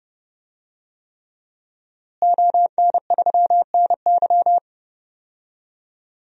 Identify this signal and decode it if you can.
Morse code, heard as a keyed tone, about 29 wpm, 700 Hz. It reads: ON3NY
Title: DXCC-100 Call Signs - 5 Characters - Repeated 3x - 30wpm